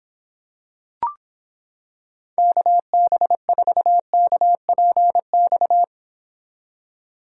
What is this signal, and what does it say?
Morse code, heard as a keyed tone, about 26 wpm, 700 Hz. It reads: KB4KPX